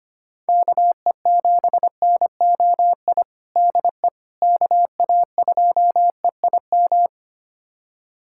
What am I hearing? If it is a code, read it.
Morse code, heard as a keyed tone, about 25 wpm, 700 Hz. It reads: KE7NOI DE KA2EIM
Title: USA Callsign DE Callsign - 6 Characters - 25wpm